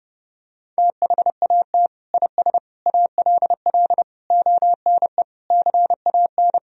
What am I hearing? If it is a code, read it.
Morse code, heard as a keyed tone, about 32 wpm, 700 Hz. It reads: THAT IS ALL ONE CAN
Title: Sentences from Top 200 Words and Abbreviations 30wpm